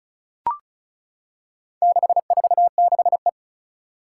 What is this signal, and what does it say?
Morse code, heard as a keyed tone, about 36 wpm, 700 Hz. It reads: 646E
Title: Four Challenging Letter-Number Combinations 35wpm